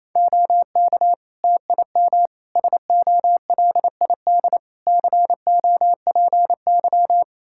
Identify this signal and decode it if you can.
Morse code, heard as a keyed tone, about 27 wpm, 700 Hz. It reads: OK TIM SOLID COPY
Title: Practice QSO - Exchange 3 - More Info - 28wpm